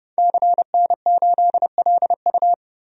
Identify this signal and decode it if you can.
Morse code, heard as a keyed tone, about 30 wpm, 700 Hz. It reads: CN8LU
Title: DXCC-100 Call Signs - 5 Characters - Repeated 2x - 30wpm